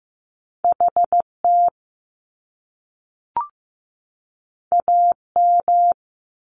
Morse code, heard as a keyed tone, about 15 wpm, 700 Hz. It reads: HT AM
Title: Top Two Letter Combinations in Words 15wpm